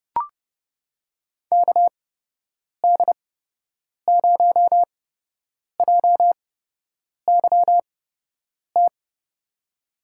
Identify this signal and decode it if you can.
Morse code, heard as a keyed tone, about 32 wpm, spaced as for 8 wpm, 700 Hz. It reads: KD0JYT
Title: USA Call Signs - 6 Characters - 8x Character Spacing - 30wpm